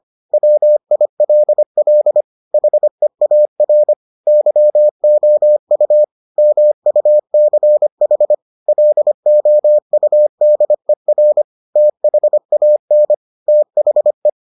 Morse code, heard as a keyed tone, about 25 wpm, 600 Hz. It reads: WILL HEAR YOU MUCH LOUDER THAN THE